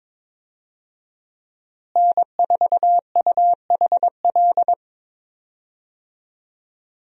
Morse code, heard as a keyed tone, about 22 wpm, 700 Hz. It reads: N4UHL
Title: USA Call Signs - 5 Characters - Repeated 3x - 22wpm